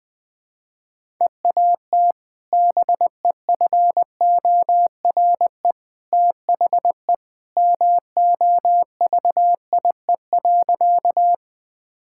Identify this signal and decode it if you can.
Morse code, heard as a keyed tone, about 20 wpm, 700 Hz. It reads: EAT BEFORE THE MOVIE.
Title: Sentences with 2-Word Phrases - Spoken Beginning - 20wpm